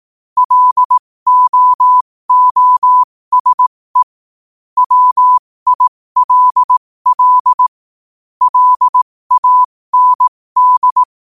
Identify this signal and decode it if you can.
Morse code, heard as a keyed tone, about 18 wpm, spaced as for 15 wpm, 1000 Hz. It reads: LOOSE WILL LAND
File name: qotdfiles / 15wpm